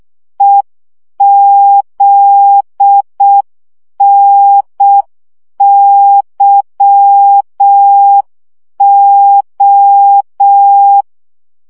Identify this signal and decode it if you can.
Morse code, heard as a keyed tone, about 6 wpm, 800 Hz. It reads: EZNYO
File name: morze vizsga 2011-3 5